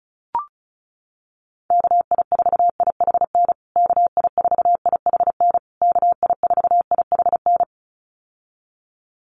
Morse code, heard as a keyed tone, about 34 wpm, 700 Hz. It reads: KI4IHN KI4IHN KI4IHN
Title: USA Call Signs Repeated 3x 35wpm